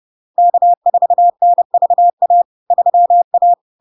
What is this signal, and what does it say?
Morse code, heard as a keyed tone, about 30 wpm, 700 Hz. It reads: K4NVA 3A
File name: LARK-FD-2022-Log-in-CW-30wpm-700-Hz